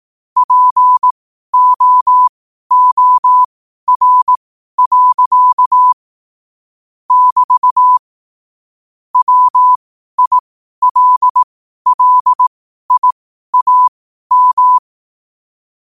Morse code, heard as a keyed tone, about 18 wpm, spaced as for 12 wpm, 1000 Hz. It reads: POOR. <BT> WILLIAM
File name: qotdfiles / 12wpm